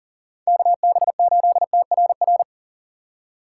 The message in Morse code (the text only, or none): KB8TRR